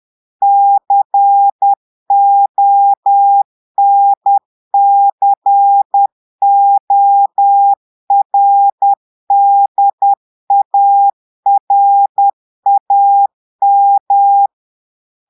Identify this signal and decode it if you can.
Morse code, heard as a keyed tone, about 10 wpm, 800 Hz. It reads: CONCORDARAM
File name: Portugais - texte en clair 1 - vitesse - 10 mots minute